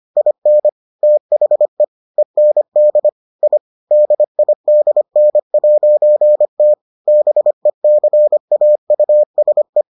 Morse code, heard as a keyed tone, about 25 wpm, 600 Hz. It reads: IN THE END I DIDN'T BECAUSE